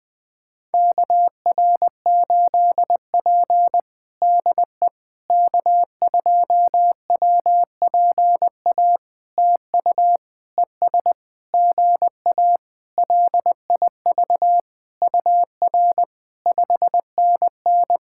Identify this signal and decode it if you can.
Morse code, heard as a keyed tone, about 20 wpm, 700 Hz. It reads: KR8P DE K2WPA TU ES GA LIV UR 5NN
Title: POTA QSO Practice - 20wpm